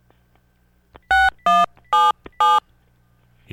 DTMF, heard as telephone touch tones, tones of about 210 ms. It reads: 6177